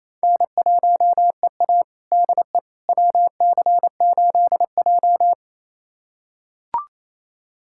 Morse code, heard as a keyed tone, about 28 wpm, 700 Hz. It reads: N1EA DE WC8J
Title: USA Callsign DE Callsign - 4 Characters - 28wpm